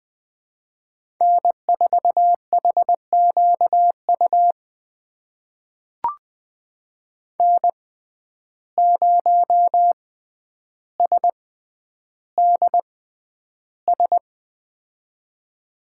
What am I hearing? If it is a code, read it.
Morse code, heard as a keyed tone, about 20 wpm, spaced as for 10 wpm, 700 Hz. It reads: N4HQU N0SDS
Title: USA Call Signs - 5 Characters - 6x Character Spacing - 20wpm